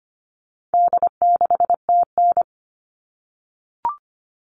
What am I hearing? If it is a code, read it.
Morse code, heard as a keyed tone, about 25 wpm, 700 Hz. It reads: D6TN
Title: All Cut Numbers - 4-Digit 25wpm